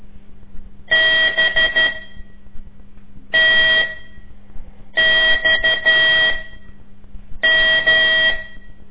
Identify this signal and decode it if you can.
Morse code, heard as a keyed tone, about 12 wpm, spaced as for 5 wpm, 1950 Hz. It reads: BTXM
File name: ik1weg2 / CORSI CW / lez5-2